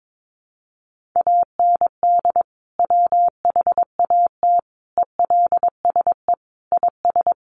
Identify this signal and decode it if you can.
Morse code, heard as a keyed tone, about 22 wpm, 700 Hz. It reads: AND WHAT ELSE IS
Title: Sentences from Top 1500 Words 22wpm